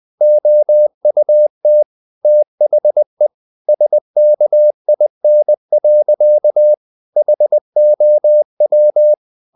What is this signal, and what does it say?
Morse code, heard as a keyed tone, about 20 wpm, 600 Hz. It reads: OUT THE SKIN. HOW
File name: the old man and the sea / the old man and the sea part 1 20wpm